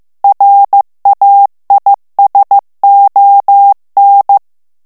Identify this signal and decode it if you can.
Morse code, heard as a keyed tone, about 15 wpm, 800 Hz. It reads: RAISON
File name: FRANCAIS TEXTE CLAIR 2 VITESSE 14 MOTS MINUTES